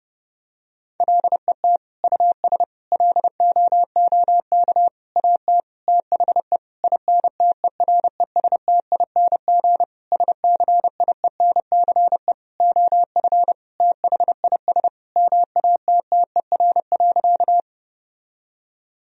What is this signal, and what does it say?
Morse code, heard as a keyed tone, about 32 wpm, 700 Hz. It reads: LET US LOOK AT THE INTERESTING SCIENCE OF THIS MATTER.
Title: Sentences from Top 1000 Words - Review Part 2 - 30wpm